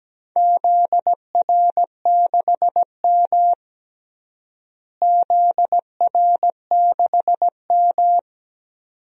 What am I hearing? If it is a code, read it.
Morse code, heard as a keyed tone, about 17 wpm, 700 Hz. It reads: ZR6M ZR6M
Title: DXCC-100 Call Signs - 4 Characters - Repeated 2x - 17wpm